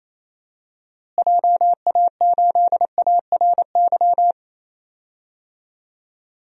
Morse code, heard as a keyed tone, about 28 wpm, 700 Hz. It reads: JA8ARY
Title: DXCC-100 Call Signs - 6 Characters - Repeated 2x - 28wpm